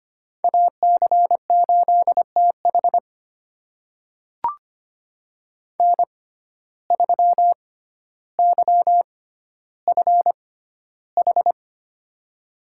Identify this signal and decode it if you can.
Morse code, heard as a keyed tone, about 25 wpm, spaced as for 13 wpm, 700 Hz. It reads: AC8TH N3YFH
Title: USA Call Signs - 6x Character Spacing - 25wpm